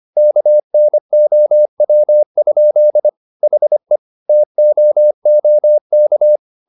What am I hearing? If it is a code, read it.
Morse code, heard as a keyed tone, about 25 wpm, 600 Hz. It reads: KNOW? HE TOOK
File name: the old man and the sea part 1 25wpm